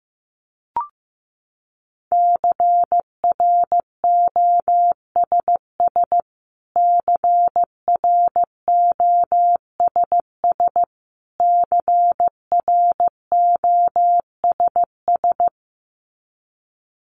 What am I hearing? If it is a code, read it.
Morse code, heard as a keyed tone, about 15 wpm, 700 Hz. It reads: CROSS CROSS CROSS
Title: Top 300 Words Repeated 15wpm